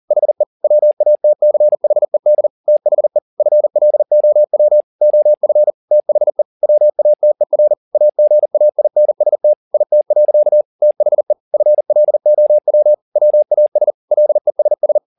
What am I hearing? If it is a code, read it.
Morse code, heard as a keyed tone, about 40 wpm, 600 Hz. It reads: HE WATCHED THE FLOW OF THE WATER AGAINST IT. THE FLOW WAS LESS
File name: the old man and the sea 2 40wpm